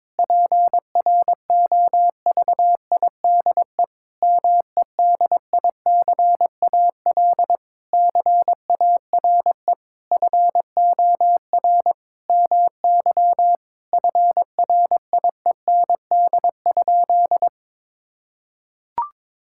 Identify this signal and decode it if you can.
Morse code, heard as a keyed tone, about 21 wpm, 700 Hz. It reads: PROVIDE MEDICAL CARE FOR MY FRIEND?
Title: Sentences from Top 1000 Words - Review Part 5 - 22wpm